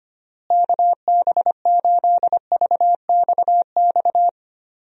Morse code, heard as a keyed tone, about 25 wpm, 700 Hz. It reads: KB8VXX
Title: USA Call Signs - 6 Characters - Repeated 2x - 25wpm